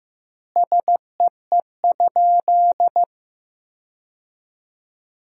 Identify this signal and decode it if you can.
Morse code, heard as a keyed tone, about 15 wpm, 700 Hz. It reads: SEE?
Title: 2x Word-Spacing - Sentences from Top 100 Words - 15wpm